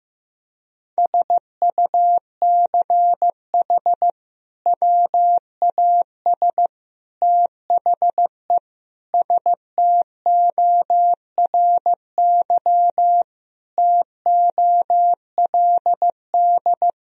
Morse code, heard as a keyed tone, about 16 wpm, 700 Hz. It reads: SUCH WAS THE STORY TOLD
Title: Sentences from Top 1000 Words - Review Part 3 - 15wpm